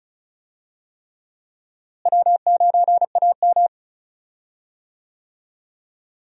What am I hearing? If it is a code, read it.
Morse code, heard as a keyed tone, about 36 wpm, 700 Hz. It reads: W9AM
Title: USA Call Signs - 4 Characters - Repeated 3x - 35wpm